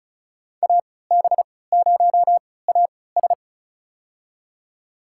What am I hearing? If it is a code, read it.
Morse code, heard as a keyed tone, about 34 wpm, spaced as for 20 wpm, 700 Hz. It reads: AB0AS
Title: USA Call Signs - 3x Character Spacing - 35wpm